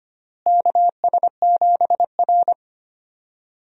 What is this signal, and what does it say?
Morse code, heard as a keyed tone, about 25 wpm, 700 Hz. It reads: KS7R
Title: USA Call Signs - 4 Characters - Repeated 3x - 25wpm